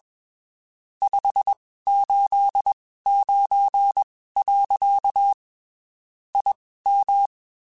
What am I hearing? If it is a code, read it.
Morse code, heard as a keyed tone, about 21 wpm, spaced as for 15 wpm, 800 Hz. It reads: 589. IM